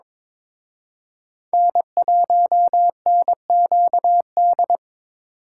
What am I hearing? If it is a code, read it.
Morse code, heard as a keyed tone, about 26 wpm, 700 Hz. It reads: N1NQD
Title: Speed Racing - USA 1x3 Call Signs 22wpm